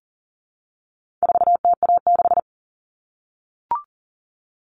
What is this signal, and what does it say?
Morse code, heard as a keyed tone, about 40 wpm, 700 Hz. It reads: 4TA6
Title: All Cut Numbers - 4-Digit 40wpm